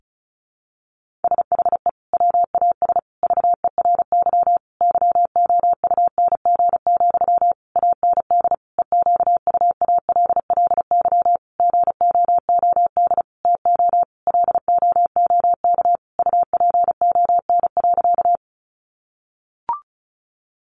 Morse code, heard as a keyed tone, about 34 wpm, 700 Hz. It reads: SHE WAS VERY YOUNG, AND EQUALLY GOOD TO LOOK UPON.
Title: Sentences from Top 1400 Words 35wpm